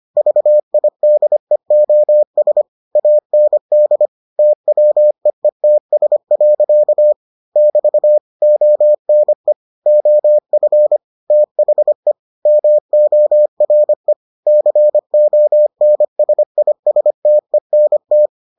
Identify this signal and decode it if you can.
Morse code, heard as a keyed tone, about 25 wpm, 600 Hz. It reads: VIDEOS AND TWEETS. <BT> ONE OF THE MORE CONSISTENT